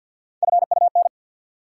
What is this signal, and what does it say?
Morse code, heard as a keyed tone, about 53 wpm, 700 Hz. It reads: FUN
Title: Sentences from Top 700 Words - New Word Drill - 50wpm